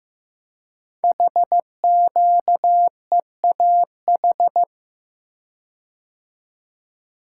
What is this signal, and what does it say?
Morse code, heard as a keyed tone, about 15 wpm, 700 Hz. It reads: HQEAH
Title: Code Groups - 5 Letters - 15wpm